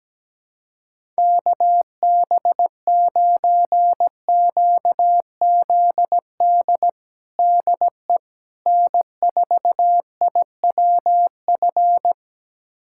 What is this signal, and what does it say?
Morse code, heard as a keyed tone, about 17 wpm, 700 Hz. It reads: KB9QZD DE N4IWF
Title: USA Callsign DE Callsign 17wpm